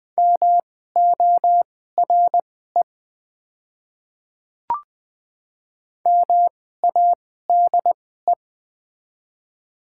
Morse code, heard as a keyed tone, about 20 wpm, spaced as for 14 wpm, 700 Hz. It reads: MORE MADE
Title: 2x Character Spacing - Top 200 Words - 20wpm